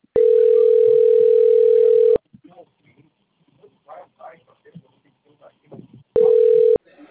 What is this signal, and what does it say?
Ringback tone (North America), 440 + 480 Hz, with speech.
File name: rg-1600-unknown- - - .1731